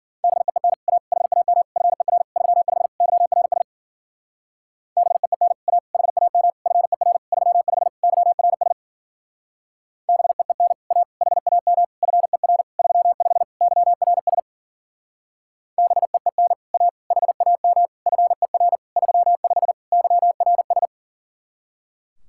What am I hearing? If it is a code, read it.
Morse code, heard as a keyed tone, about 50 wpm, 700 Hz. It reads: BEEN A HAM FER 35 YRS BEEN A HAM FER 35 YRS BEEN A HAM FER 35 YRS BEEN A HAM FER 35 YRS
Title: Speed-Racing Common QSO Phrases - 40wpm